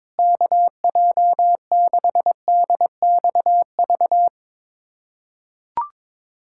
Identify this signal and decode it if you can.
Morse code, heard as a keyed tone, about 22 wpm, 700 Hz. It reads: KJ6DXV
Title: USA Callsign DE Callsign 22wpm